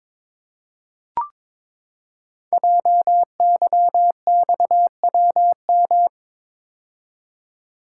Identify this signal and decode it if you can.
Morse code, heard as a keyed tone, about 22 wpm, 700 Hz. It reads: JYXWM
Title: Code Groups - 5 Letters - 22wpm